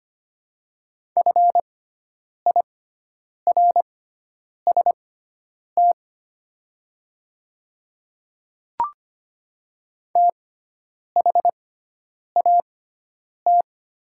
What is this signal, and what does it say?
Morse code, heard as a keyed tone, about 25 wpm, spaced as for 9 wpm, 700 Hz. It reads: FIRST THAT
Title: 6x Character Spacing - Top 100 Words - 25wpm